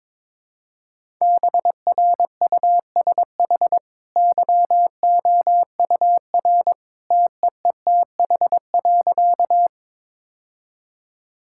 Morse code, heard as a keyed tone, about 22 wpm, 690 Hz. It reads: BRUSH YOUR TEETH.
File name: practice / 22wpm / Easy Sentences with Hints - 22wpm